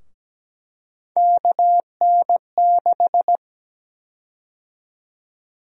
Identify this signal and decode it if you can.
Morse code, heard as a keyed tone, about 17 wpm, 700 Hz. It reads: KN6
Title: USA Call Sign Prefixes - Repeated 2x - 17wpm